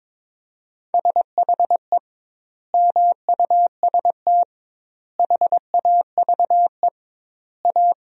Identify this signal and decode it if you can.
Morse code, heard as a keyed tone, about 21 wpm, 700 Hz. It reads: SHE MUST HAVE A
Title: 2x Word-Spacing - Sentences from Top 300 Words - 22wpm